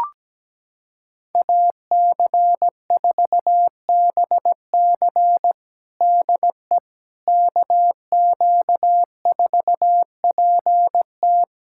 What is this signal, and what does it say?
Morse code, heard as a keyed tone, about 17 wpm, 700 Hz. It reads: AC4BC DE KQ4PT
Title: USA Callsign DE Callsign - 5 Characters - 17wpm